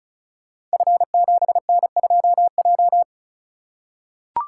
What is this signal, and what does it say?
Morse code, heard as a keyed tone, about 36 wpm, 700 Hz. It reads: F7N2J
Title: Code Groups - 5 Letter:Numbers - 35wpm